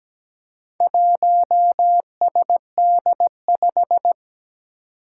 Morse code, heard as a keyed tone, about 17 wpm, 700 Hz. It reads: 1SD5
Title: Four Challenging Letter-Number Combinations 17wpm